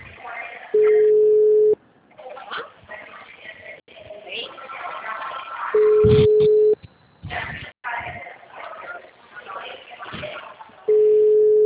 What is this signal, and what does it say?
Ringback tone (Europe, 425 Hz), with speech.
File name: out- -2025- - - .43062